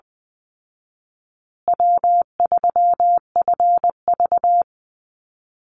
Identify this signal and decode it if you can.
Morse code, heard as a keyed tone, about 19 wpm, 700 Hz. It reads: W3FV
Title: Easy Call Signs Repeated 3x - 20wpm